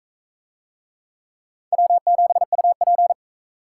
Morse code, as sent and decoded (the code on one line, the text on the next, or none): .-- --... ..- .--.
W7UP